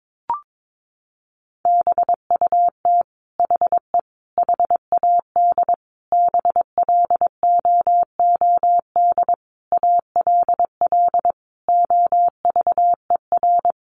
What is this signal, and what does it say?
Morse code, heard as a keyed tone, about 22 wpm, 700 Hz. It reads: BUT HE HAD BLOOD ALL OVER
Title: Sentences from Top 1100 Words 22wpm